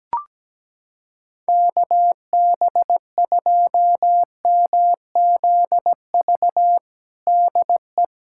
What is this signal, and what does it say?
Morse code, heard as a keyed tone, about 17 wpm, 700 Hz. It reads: KB2MZV DE
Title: USA Callsign DE Callsign - 6 Characters - 17wpm